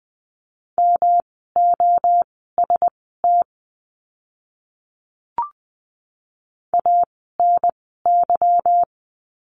Morse code, heard as a keyed tone, about 20 wpm, spaced as for 15 wpm, 700 Hz. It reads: MOST ANY
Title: 192 Lesson 31f - Letter v - Words 20wpm